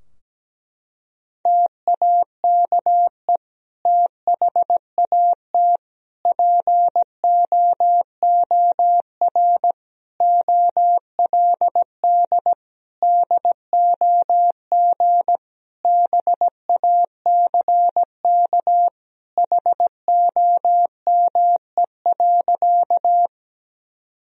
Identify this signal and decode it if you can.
Morse code, heard as a keyed tone, about 17 wpm, 700 Hz. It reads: TAKE THAT POOR OLD DOG BACK HOME.
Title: Sentences from Top 1000 Words - Review Part 2 - 17wpm